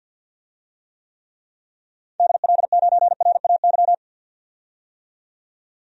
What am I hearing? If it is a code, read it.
Morse code, heard as a keyed tone, about 53 wpm, 700 Hz. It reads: DL9RAY
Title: DXCC-100 Call Signs - 6 Characters - 50wpm